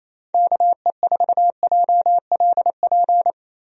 Morse code, heard as a keyed tone, about 27 wpm, 700 Hz. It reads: KE4JLP
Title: USA Call Signs - 6 Characters - Repeated 3x - 28wpm